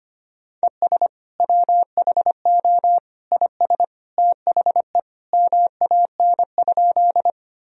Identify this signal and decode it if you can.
Morse code, heard as a keyed tone, about 25 wpm, 700 Hz. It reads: ES WHO IS THE MAN?
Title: Sentences from Top 200 Words and Abbreviations 25wpm